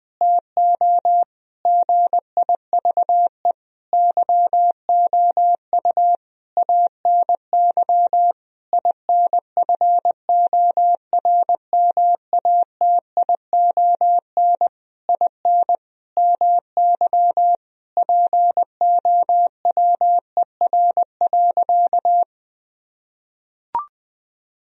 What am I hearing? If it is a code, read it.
Morse code, heard as a keyed tone, about 20 wpm, 700 Hz. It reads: TO GIVE YOU ANY INFORMATION IN MY POWER.
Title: Sentences from Top 1000 Words - Review Part 3 - 20wpm